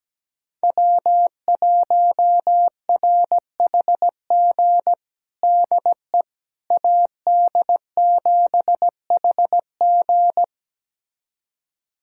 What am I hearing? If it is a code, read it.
Morse code, heard as a keyed tone, about 17 wpm, 700 Hz. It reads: W1RHG DE AD7HG
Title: USA Callsign DE Callsign - 5 Characters - 17wpm